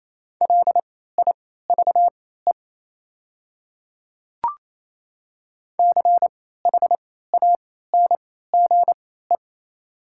Morse code, heard as a keyed tone, about 28 wpm, spaced as for 16 wpm, 700 Hz. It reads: LIVE CHANGE
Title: 3x Character Spacing - Top 200 Words - 28wpm